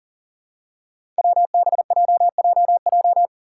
Morse code, heard as a keyed tone, about 40 wpm, 700 Hz. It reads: WBJJJ
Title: Five Challenging Letter-Number Combinations 40wpm